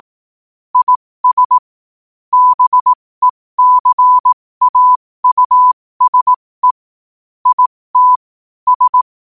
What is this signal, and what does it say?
Morse code, heard as a keyed tone, about 18 wpm, spaced as for 15 wpm, 1000 Hz. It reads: IS BECAUSE IT S
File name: qotdfiles / 15wpm